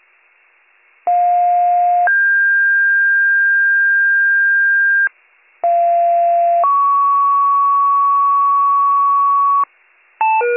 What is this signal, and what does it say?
Two-tone page, heard as a one-second tone, then a three-second tone, about 710 then 1600 Hz, about 690 then 1120 Hz, with noise. It is heard from 1.1 to 9.6 s.